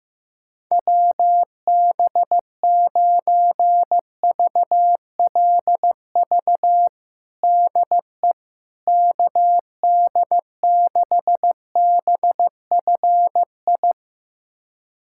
Morse code, heard as a keyed tone, about 15 wpm, 700 Hz. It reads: WB9VLV DE KD6BFI
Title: USA Callsign DE Callsign 15wpm